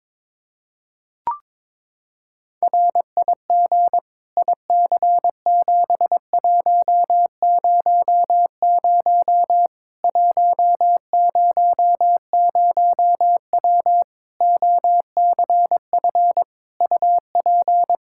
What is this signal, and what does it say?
Morse code, heard as a keyed tone, about 22 wpm, 700 Hz. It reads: RIG IC7100 100W OCF UP
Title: Practice QSO - Exchange 2 - Info - 22wpm